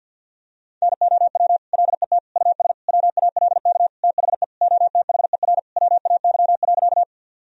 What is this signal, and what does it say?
Morse code, heard as a keyed tone, about 52 wpm, 700 Hz. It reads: NOW LET US WALK THE OTHER WAY.
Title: Sentences from Top 1000 Words - Review Part 1 - 50wpm